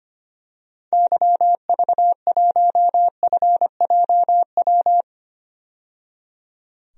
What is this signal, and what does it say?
Morse code, heard as a keyed tone, about 25 wpm, 700 Hz. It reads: YV1FJW
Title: DXCC-100 Call Signs - 6 Characters - 25wpm